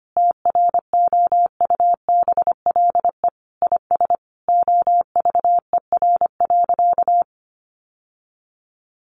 Morse code, heard as a keyed tone, about 25 wpm, 700 Hz. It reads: TROUBLE IS OVER.
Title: Sentences from Top 1200 Words 25wpm